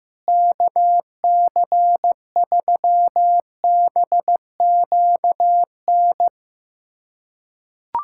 Morse code, heard as a keyed tone, about 15 wpm, 700 Hz. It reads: KC3BQN
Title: USA Callsign DE Callsign - 6 Characters - 15wpm